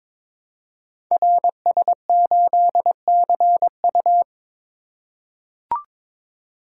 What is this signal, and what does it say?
Morse code, heard as a keyed tone, about 22 wpm, 700 Hz. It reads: RS8CU
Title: Code Groups - 5 Letter:Numbers - 22wpm